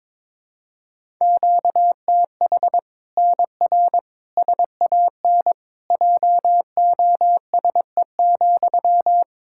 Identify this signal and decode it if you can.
Morse code, heard as a keyed tone, about 22 wpm, 700 Hz. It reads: QTH NR SAN JOSE,
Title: Practice QSO - Exchange 1 - Beginning - 22wpm